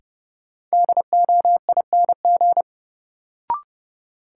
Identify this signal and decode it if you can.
Morse code, heard as a keyed tone, about 31 wpm, 700 Hz. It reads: DOING
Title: Sets of 4 Words - New Word Drill - 30wpm